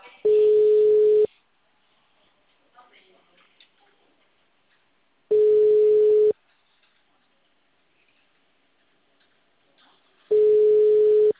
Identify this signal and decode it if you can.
Ringback tone (Europe, 425 Hz), with speech.
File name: out- -2012- - - .15460